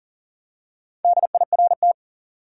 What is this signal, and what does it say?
Morse code, heard as a keyed tone, about 40 wpm, 700 Hz. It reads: DIRT
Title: Four Letter Words - 40wpm